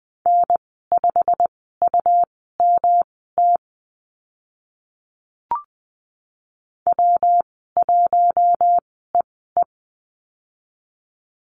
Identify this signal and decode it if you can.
Morse code, heard as a keyed tone, about 20 wpm, spaced as for 14 wpm, 700 Hz. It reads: N5UMT W1EE
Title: 140 Lesson 23g - Letter f - Call Signs 20wpm